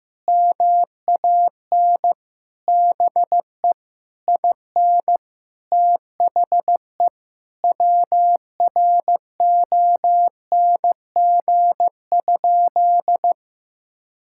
Morse code, heard as a keyed tone, about 15 wpm, 700 Hz. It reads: MAN BE IN THE WRONG?
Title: Sentences from Top 1000 Words - Review Part 6 - 15wpm